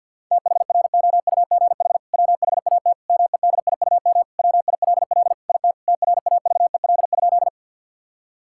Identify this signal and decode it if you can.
Morse code, heard as a keyed tone, about 52 wpm, 700 Hz. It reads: THROUGH WHAT MEDIUM WILL IT TRAVEL?
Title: Sentences from Top 1000 Words - Review Part 6 - 50wpm